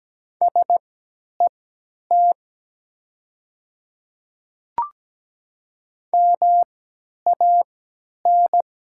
Morse code, heard as a keyed tone, about 17 wpm, spaced as for 10 wpm, 700 Hz. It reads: SET MAN
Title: 3x Character Spacing - Top 200 Words - 17wpm